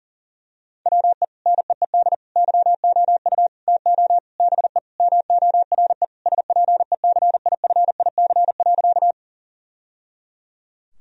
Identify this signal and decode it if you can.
Morse code, heard as a keyed tone, about 40 wpm, 700 Hz. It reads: WE NEED YOU TO BE MORE SPECIFIC.
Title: Sentences from Top 1000 Words - Review Part 3 - 40wpm